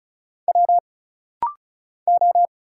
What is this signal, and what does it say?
Morse code, heard as a keyed tone, about 35 wpm, 700 Hz. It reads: W O
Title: Single Letters - Rapid-Fire 35wpm